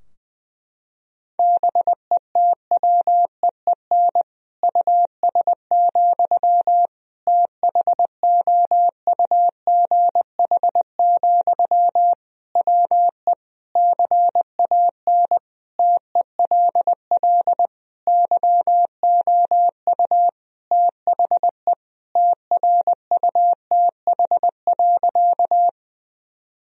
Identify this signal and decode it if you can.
Morse code, heard as a keyed tone, about 21 wpm, 700 Hz. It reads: BETWEEN US, THOUGH, WE CAN TELL YOU THE TRUTH.
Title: Sentences from Top 1000 Words - Review Part 2 - 20wpm